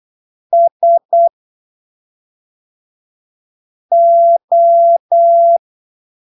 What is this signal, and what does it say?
Morse code, heard as a keyed tone, about 8 wpm, spaced as for 3 wpm, 680 Hz. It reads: SO
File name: arquivos / texto / texto 13 08ppm